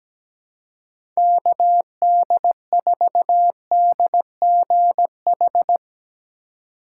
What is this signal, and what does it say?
Morse code, heard as a keyed tone, about 17 wpm, 700 Hz. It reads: KD4DGH
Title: USA Call Signs - 6 Characters - 3x Character Spacing - 17wpm